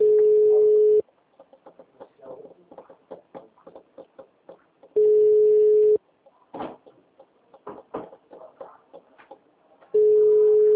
Ringback tone (Europe, 425 Hz).
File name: out- -2005- - - .62370